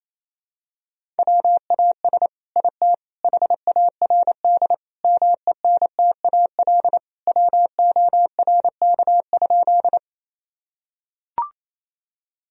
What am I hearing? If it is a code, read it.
Morse code, heard as a keyed tone, about 28 wpm, 700 Hz. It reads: WAS IT HARD MENTAL WORK?
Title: Sentences from Top 1000 Words - Review Part 4 - 28wpm